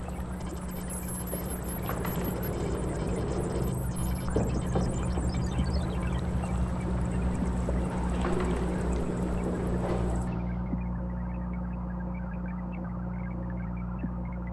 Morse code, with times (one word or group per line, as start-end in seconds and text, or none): none